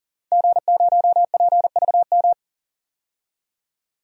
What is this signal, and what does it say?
Morse code, heard as a keyed tone, about 40 wpm, 700 Hz. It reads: G0PVM
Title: DXCC-100 Call Signs - 5 Characters - 40wpm